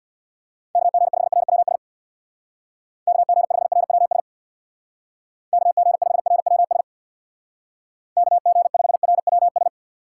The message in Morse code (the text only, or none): KG5RWS KG5RWS KG5RWS KG5RWS